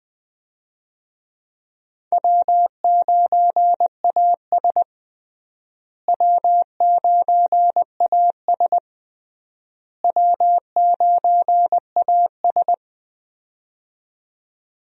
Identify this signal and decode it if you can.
Morse code, heard as a keyed tone, about 21 wpm, 700 Hz. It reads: W9AS W9AS W9AS
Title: USA Call Signs - 4 Characters - Repeated 3x - 20wpm